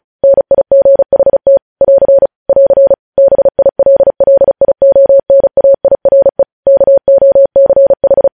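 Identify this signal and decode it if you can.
Morse code, heard as a keyed tone, about 36 wpm, 550 Hz. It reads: NIGHT <AR> <AR> BILLIONAIRE KOCH
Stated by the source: Morse practice sending news headlines at 35 wpm.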